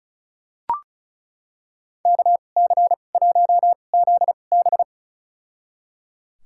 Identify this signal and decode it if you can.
Morse code, heard as a keyed tone, about 36 wpm, spaced as for 25 wpm, 700 Hz. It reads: KC1ZB
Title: USA Call Signs - 2x Character Spacing - 35wpm